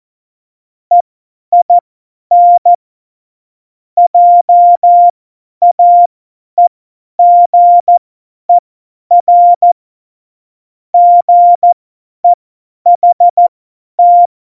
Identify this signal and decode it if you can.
Morse code, heard as a keyed tone, about 14 wpm, spaced as for 10 wpm, 700 Hz. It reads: EIN JAEGER GEHT
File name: Morsekurs / Witze / Witz 27 Juni 17, Tempo 50 F70